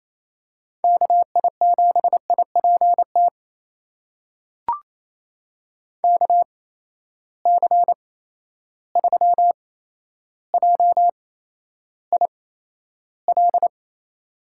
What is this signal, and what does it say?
Morse code, heard as a keyed tone, about 28 wpm, spaced as for 12 wpm, 700 Hz. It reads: KI7IPT KC3JIL
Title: USA Call Signs - 6 Characters - 8x Character Spacing - 28wpm